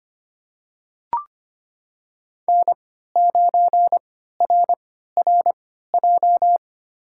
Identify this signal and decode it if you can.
Morse code, heard as a keyed tone, about 25 wpm, spaced as for 14 wpm, 700 Hz. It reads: N9RRJ
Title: USA Call Signs - 3x Character Spacing - 25wpm